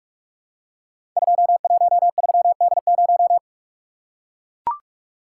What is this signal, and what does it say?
Morse code, heard as a keyed tone, about 44 wpm, 700 Hz. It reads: 213D0
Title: Code Groups - 5 Letter:Numbers - 45wpm